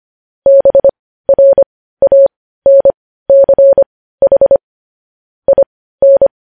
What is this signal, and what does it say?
Morse code, heard as a keyed tone, about 25 wpm, spaced as for 15 wpm, 550 Hz. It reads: BRANCH IN
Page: news headlines in morse code at 15 wpm